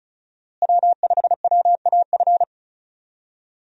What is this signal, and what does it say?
Morse code, heard as a keyed tone, about 34 wpm, 700 Hz. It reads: W5WAF
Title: USA Call Signs - 5 Characters - Repeated 3x - 35wpm